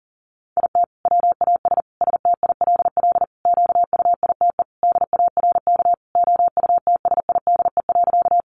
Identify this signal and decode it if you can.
Morse code, heard as a keyed tone, about 40 wpm, 700 Hz. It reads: IT WAS STILL QUITE DARK OUTSIDE.
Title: Sentences from Top 900 Words 40wpm